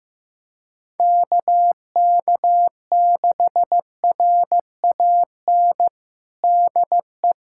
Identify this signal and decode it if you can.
Morse code, heard as a keyed tone, about 15 wpm, 700 Hz. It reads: KK6RAN DE
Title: USA Callsign DE Callsign - 6 Characters - 15wpm